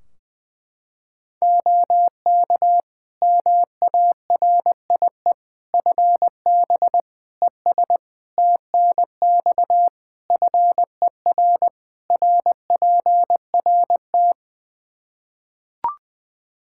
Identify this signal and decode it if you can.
Morse code, heard as a keyed tone, about 20 wpm, 700 Hz. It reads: OK MARIE FB ES TNX FER RPRT
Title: Practice QSO - Exchange 2 - Info - 20wpm